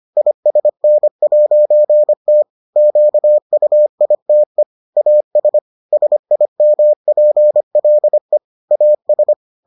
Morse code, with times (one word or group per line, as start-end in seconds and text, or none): 0.2-2.4 ISN'T
2.8-4.6 QUITE
5.0-5.6 AS
5.9-8.4 SIMPLE
8.7-9.3 AS